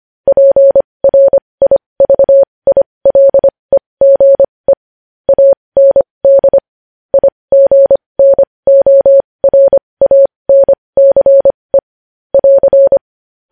Morse code, heard as a keyed tone, about 25 wpm, spaced as for 20 wpm, 550 Hz. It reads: PRIVILEGE AND IGNORANCE <AR>
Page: news headlines in morse code at 20 wpm